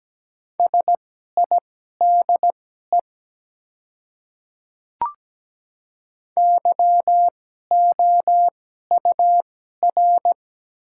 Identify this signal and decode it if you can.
Morse code, heard as a keyed tone, about 17 wpm, spaced as for 12 wpm, 700 Hz. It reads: SIDE YOUR
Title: 2x Character Spacing - Top 200 Words - 17wpm